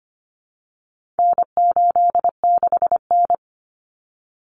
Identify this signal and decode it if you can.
Morse code, heard as a keyed tone, about 25 wpm, 700 Hz. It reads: N86N
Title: Common Cut Numbers - 4-Digit 25wpm